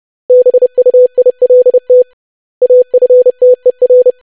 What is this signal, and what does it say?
Morse code, heard as a keyed tone, about 30 wpm, 500 Hz. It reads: BUILT AFTER
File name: lesson / web / basic500-25wpm